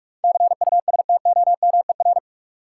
Morse code, heard as a keyed tone, about 47 wpm, 700 Hz. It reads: CUSTOMER
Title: Sentences from Top 1100 Words - New Word Drill - 45wpm